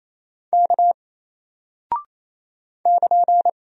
Morse code, heard as a keyed tone, about 28 wpm, 700 Hz. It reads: K <KN>
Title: Single Character (Letter:Number:Punctuation:Prosign) Alt 28wpm